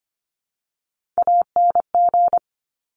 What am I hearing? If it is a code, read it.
Morse code, heard as a keyed tone, about 25 wpm, 700 Hz. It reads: ANG
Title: Top Three Letter Combinations in Words 25wpm